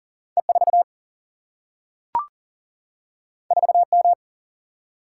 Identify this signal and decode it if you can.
Morse code, heard as a keyed tone, about 40 wpm, 700 Hz. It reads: E4 4M
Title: Single-Letter Single-Number 40wpm